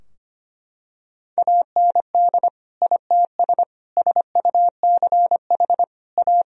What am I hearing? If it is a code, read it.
Morse code, heard as a keyed tone, about 25 wpm, 700 Hz. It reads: AND ITS SUCH A
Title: Sentences from Top 1000 Words - Review Part 2 - 25wpm